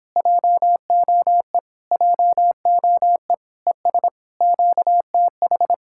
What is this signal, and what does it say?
Morse code, heard as a keyed tone, about 26 wpm, 700 Hz. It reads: JOE JOE ES QTH